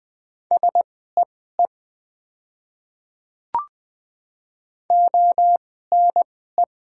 Morse code, heard as a keyed tone, about 20 wpm, spaced as for 14 wpm, 690 Hz. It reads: SEE ONE
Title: 2x Character Spacing - Top 100 Words - 20wpm